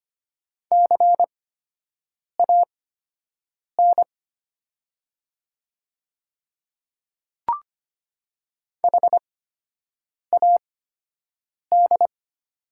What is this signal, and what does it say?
Morse code, heard as a keyed tone, about 25 wpm, spaced as for 7 wpm, 700 Hz. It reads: CAN HAD
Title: 8x Character Spacing - Top 300 Words - 25wpm